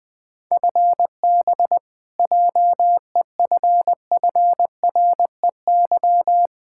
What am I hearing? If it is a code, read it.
Morse code, heard as a keyed tone, about 20 wpm, 700 Hz. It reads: FB JEFFREY
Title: Practice QSO - Exchange 3 - More Info - 20wpm